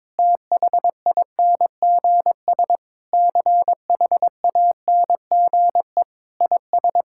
Morse code, heard as a keyed tone, about 22 wpm, 700 Hz. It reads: THINGS CHANGE IS
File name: practice / 22wpm / Sentences with 8-Word Phrases - 22wpm